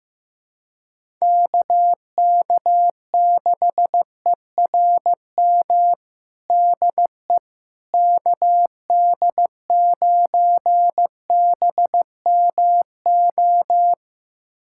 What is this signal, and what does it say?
Morse code, heard as a keyed tone, about 15 wpm, 700 Hz. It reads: KK6ERM DE KD9BMO
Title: USA Callsign DE Callsign - 6 Characters - 15wpm